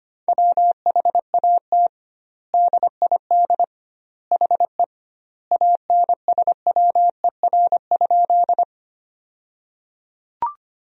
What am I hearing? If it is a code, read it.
Morse code, heard as a keyed tone, about 25 wpm, 700 Hz. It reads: WHAT DID HE ANSWER?
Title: 2x Word-Spacing - Sentences from Top 300 Words - 25wpm